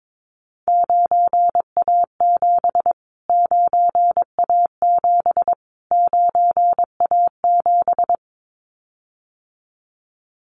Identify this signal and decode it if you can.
Morse code, heard as a keyed tone, about 22 wpm, 700 Hz. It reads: 9A7 9A7 9A7
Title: Callsign Prefixes for Top 25 DXCC - Repeated 3x - 22wpm